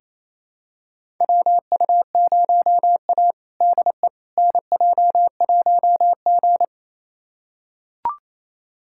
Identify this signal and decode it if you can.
Morse code, heard as a keyed tone, about 28 wpm, 700 Hz. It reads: WU0A DE NJ1G
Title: USA Callsign DE Callsign - 4 Characters - 28wpm